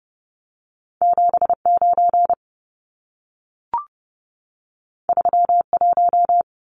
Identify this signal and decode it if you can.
Morse code, heard as a keyed tone, about 30 wpm, 700 Hz. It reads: 79 31
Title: Numbers - 2-Digit 30wpm